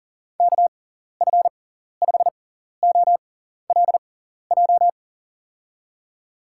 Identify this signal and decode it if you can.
Morse code, heard as a keyed tone, about 40 wpm, spaced as for 14 wpm, 700 Hz. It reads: KF5OLJ
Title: USA Call Signs - 6 Characters - 6x Character Spacing - 40wpm